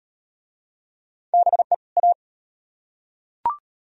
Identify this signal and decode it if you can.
Morse code, heard as a keyed tone, about 36 wpm, 700 Hz. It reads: BE A